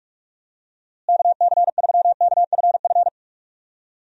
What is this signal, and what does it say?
Morse code, heard as a keyed tone, about 46 wpm, 700 Hz. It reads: KC3KFF
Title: USA Call Signs - 6 Characters - Repeated 2x - 45wpm